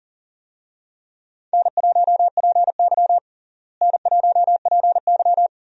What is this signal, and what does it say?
Morse code, heard as a keyed tone, about 40 wpm, 700 Hz. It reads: N1PY N1PY
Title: USA Call Signs - 4 Characters - Repeated 2x - 40wpm